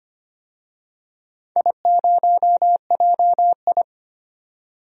Morse code, heard as a keyed tone, about 25 wpm, 700 Hz. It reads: I0JI
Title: DXCC-100 Call Signs - 4 Characters - 25wpm